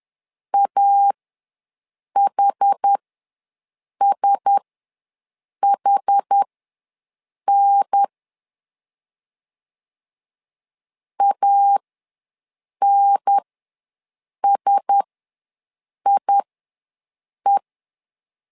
Morse code, heard as a keyed tone, about 10 wpm, spaced as for 6 wpm, 800 Hz. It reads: AHSHN ANSIE
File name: LEZIONI / 4 Q1S ws 10wpm 2208